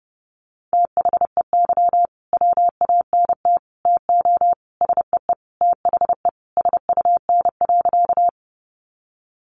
Morse code, heard as a keyed tone, about 28 wpm, 700 Hz. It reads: THEY WANT TO SEE THE SUN.
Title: Easy Phrases 30wpm